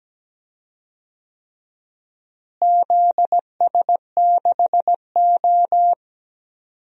Morse code, heard as a keyed tone, about 17 wpm, 700 Hz. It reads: ZS6O
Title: DXCC-100 Call Signs - 4 Characters - Repeated 3x - 17wpm